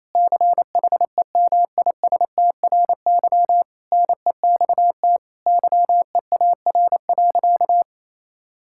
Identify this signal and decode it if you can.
Morse code, heard as a keyed tone, about 28 wpm, 700 Hz. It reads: CHEMISTRY NEXT YEAR.
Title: Sentences from Top 1000 Words - Review Part 6 - 28wpm